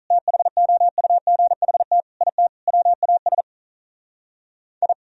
Morse code, heard as a keyed tone, about 42 wpm, 700 Hz. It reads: THOUGHT IT WAS I
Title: Speed-Racing - Sets of 4 Words 35wpm